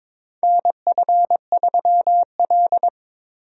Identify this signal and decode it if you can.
Morse code, heard as a keyed tone, about 22 wpm, 700 Hz. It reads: NF3L
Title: USA Call Signs - 4 Characters - Repeated 2x - 22wpm